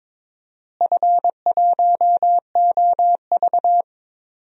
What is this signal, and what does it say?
Morse code, heard as a keyed tone, about 21 wpm, 700 Hz. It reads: F1OV
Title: DXCC-100 Call Signs - 4 Characters - Repeated 3x - 22wpm